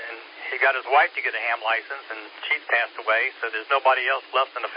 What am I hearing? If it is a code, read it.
Emergency Alert System (SAME) tones, heard as harsh buzzing bursts, with speech. It is heard from 0.3 to 3.6 s.